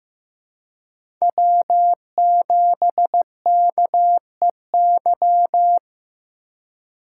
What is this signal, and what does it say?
Morse code, heard as a keyed tone, about 15 wpm, 700 Hz. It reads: W7KEY
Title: USA Call Signs - 5 Characters - 3x Character Spacing - 15wpm